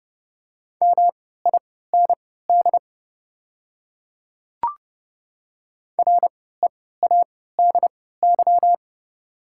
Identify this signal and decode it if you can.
Morse code, heard as a keyed tone, about 30 wpm, spaced as for 17 wpm, 690 Hz. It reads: MIND READY
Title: 3x Character Spacing - Top 500 Words - 30wpm